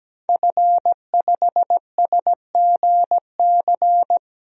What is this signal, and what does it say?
Morse code, heard as a keyed tone, about 17 wpm, 700 Hz. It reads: F5SGC
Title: DXCC-100 Call Signs - 5 Characters - 17wpm